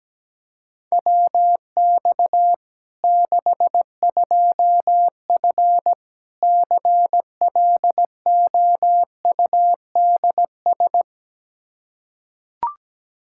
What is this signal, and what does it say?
Morse code, heard as a keyed tone, about 17 wpm, 700 Hz. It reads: WX 62F CLOUDS
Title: K1USN SST QSO Practice 17wpm